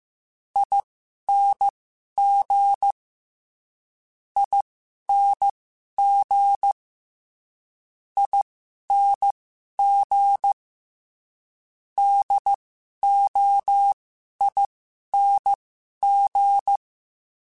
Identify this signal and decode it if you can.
Morse code, heard as a keyed tone, about 15 wpm, spaced as for 11 wpm, 800 Hz. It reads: ING ING ING DOING
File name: Special INGs-mp3s / Special INGs-mp3s / Sngl Ing @10